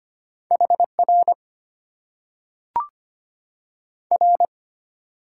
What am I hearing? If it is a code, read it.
Morse code, heard as a keyed tone, about 25 wpm, 690 Hz. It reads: HR R
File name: Top 90 QSO Elements - 6x Character Spacing - 25wpm